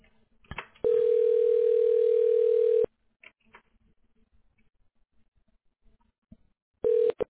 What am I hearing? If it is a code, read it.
Ringback tone (North America), 440 + 480 Hz.